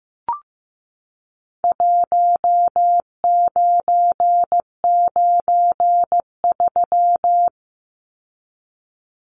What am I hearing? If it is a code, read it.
Morse code, heard as a keyed tone, about 15 wpm, 700 Hz. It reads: 1993
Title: Numbers - Contest Style - 4-Digit 15wpm